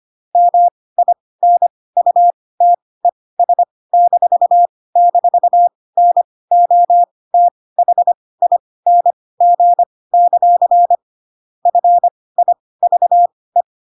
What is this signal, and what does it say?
Morse code, heard as a keyed tone, about 25 wpm, spaced as for 18 wpm, 700 Hz. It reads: MINUTES--NOTHING; FIVE
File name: Pinocchio / Pinocchio ch1 12 25 ik1hge